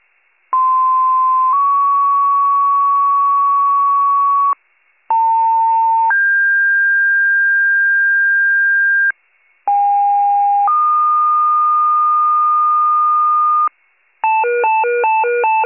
Two-tone page, heard as a one-second tone, then a three-second tone, about 1040 then 1120 Hz, about 900 then 1600 Hz, about 810 then 1220 Hz. It is heard from 0.5 to 13.7 s.